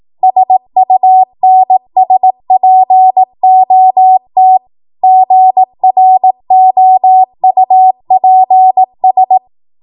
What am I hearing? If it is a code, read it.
Morse code, heard as a keyed tone, about 18 wpm, 750 Hz. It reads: SUNSPOT GROUPS
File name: Morse / 18WPM